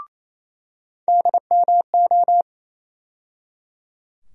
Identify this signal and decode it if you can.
Morse code, heard as a keyed tone, about 28 wpm, 700 Hz. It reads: DMO
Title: USA Call Sign Postfixes - 3 Letters 28wpm